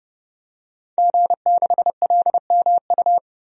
Morse code, heard as a keyed tone, about 32 wpm, 700 Hz. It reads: G6LMU
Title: DXCC-100 Call Signs - 5 Characters - Repeated 3x - 30wpm